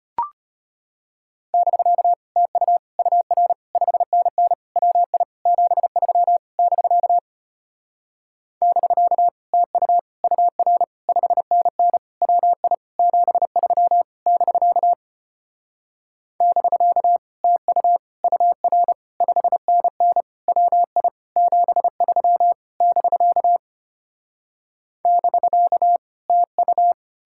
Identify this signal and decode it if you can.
Morse code, heard as a keyed tone, about 33 wpm, 700 Hz. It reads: <BK> TU UR 5NN WI 73 <BK> <BK> TU UR 5NN WI 73 <BK> <BK> TU UR 5NN WI 73 <BK> <BK> TU